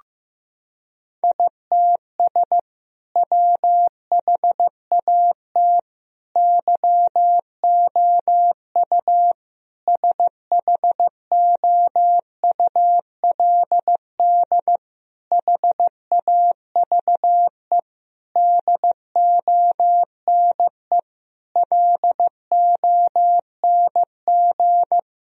Morse code, heard as a keyed tone, about 15 wpm, 700 Hz. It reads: ITS WHAT YOU SHOULD HAVE DONE LONG
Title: Sentences from Top 1000 Words - Review Part 3 - 15wpm